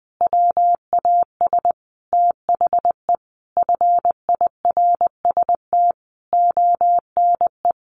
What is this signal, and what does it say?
Morse code, heard as a keyed tone, about 19 wpm, 700 Hz. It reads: WAS THE FIRST ONE
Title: Sentences from Top 1300 Words 20wpm